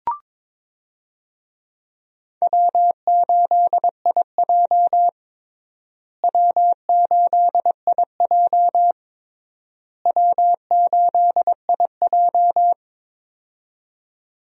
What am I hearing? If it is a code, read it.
Morse code, heard as a keyed tone, about 22 wpm, 700 Hz. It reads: W8IJ W8IJ W8IJ
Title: USA Call Signs - 4 Characters - Repeated 3x - 22wpm